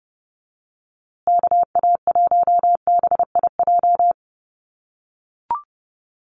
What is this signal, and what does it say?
Morse code, heard as a keyed tone, about 30 wpm, 700 Hz. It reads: KA1BIJ
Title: USA Call Signs Repeated 2x 30wpm